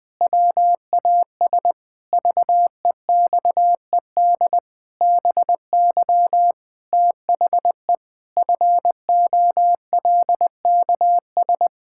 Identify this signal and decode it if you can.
Morse code, heard as a keyed tone, about 20 wpm, 700 Hz. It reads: WAS VEXED BY THE FOLKS
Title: Pangram Challenge v2 - 20wpm